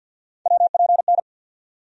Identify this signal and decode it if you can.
Morse code, heard as a keyed tone, about 52 wpm, 700 Hz. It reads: WPN
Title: USA Call Sign Postfixes - 3 Letters 50wpm